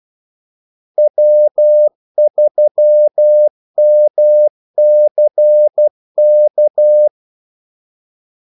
Morse code, heard as a keyed tone, about 12 wpm, 600 Hz. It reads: W3MCK
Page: 2 ekstra ordmellemrum mellem kaldesignalerne